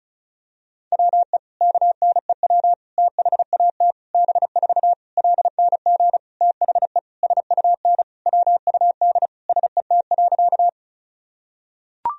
Morse code, heard as a keyed tone, about 36 wpm, 700 Hz. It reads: WE KNEW THAT B4 LNG THE SUN WUD SET.
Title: Sentences from Top 400 Words and Abbreviations 35wpm